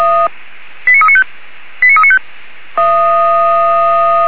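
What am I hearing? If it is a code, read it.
Five-tone selective call (ZVEI), heard as quick runs of five tones, tones of about 70 ms. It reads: 89395 89395
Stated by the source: ZVEI five-tone selective call received on SDR.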